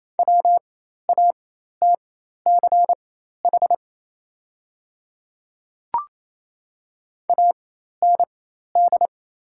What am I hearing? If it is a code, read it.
Morse code, heard as a keyed tone, about 28 wpm, spaced as for 13 wpm, 700 Hz. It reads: WATCH AND
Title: 4x Character Spacing - Top 300 Words - 28wpm